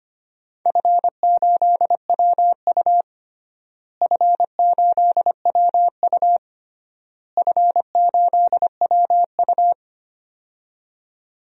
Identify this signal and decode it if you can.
Morse code, heard as a keyed tone, about 25 wpm, 700 Hz. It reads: F8WU F8WU F8WU
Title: DXCC-100 Call Signs - 4 Characters - Repeated 3x - 25wpm